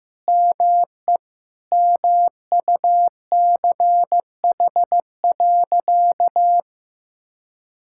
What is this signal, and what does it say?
Morse code, heard as a keyed tone, about 15 wpm, 700 Hz. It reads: ME MUCH.
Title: Sentences from Top 1000 Words - Review Part 6 - 15wpm